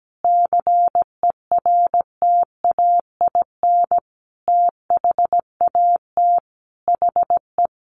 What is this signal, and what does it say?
Morse code, heard as a keyed tone, about 17 wpm, 700 Hz. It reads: CERTAIN THAT HE
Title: Sentences from Top 1500 Words 17wpm